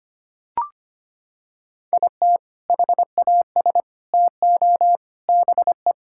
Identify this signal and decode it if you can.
Morse code, heard as a keyed tone, about 25 wpm, 700 Hz. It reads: IT HAS TO BE
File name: practice / 25wpm / Sentences with 3-Word Phrases - 25wpm